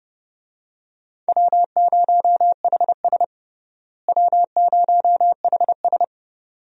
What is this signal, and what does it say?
Morse code, heard as a keyed tone, about 31 wpm, 700 Hz. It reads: W0HS W0HS
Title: USA Call Signs - 4 Characters - Repeated 2x - 30wpm